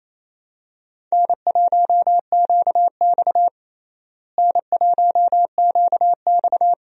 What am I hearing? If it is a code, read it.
Morse code, heard as a keyed tone, about 28 wpm, 700 Hz. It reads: N1QX N1QX
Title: USA Call Signs - 4 Characters - Repeated 3x - 28wpm